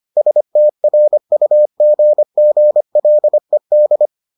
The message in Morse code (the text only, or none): STRUGGLED